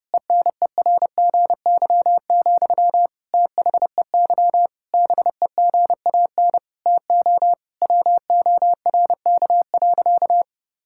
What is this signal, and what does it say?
Morse code, heard as a keyed tone, about 32 wpm, 700 Hz. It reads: ENERGY, THEY BEGAN TO WORK.
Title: Sentences from Top 1000 Words - Review Part 6 - 30wpm